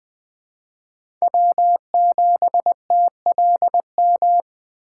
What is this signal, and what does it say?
Morse code, heard as a keyed tone, about 20 wpm, 700 Hz. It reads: W7TLM
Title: USA Call Signs - 5 Characters - 8x Character Spacing - 20wpm